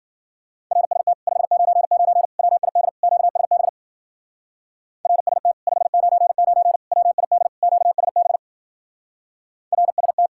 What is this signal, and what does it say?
Morse code, heard as a keyed tone, about 48 wpm, 700 Hz. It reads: RST 599 WID QSB RST 599 WID QSB RST
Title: Speed-Racing Common QSO Phrases Encore - 40wpm